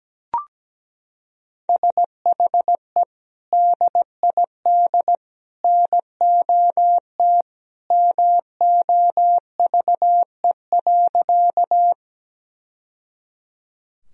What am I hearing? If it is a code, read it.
Morse code, heard as a keyed tone, about 17 wpm, 700 Hz. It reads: SHE DID NOT MOVE.
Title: Sentences from Top 1000 Words - Review Part 1 - 17wpm